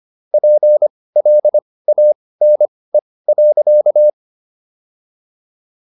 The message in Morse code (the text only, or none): PLANE.